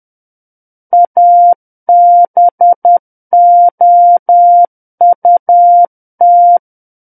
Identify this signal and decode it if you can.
Morse code, heard as a keyed tone, about 10 wpm, 700 Hz. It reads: ABOUT
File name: Book One / 01-Chapter 1 - 10 WPM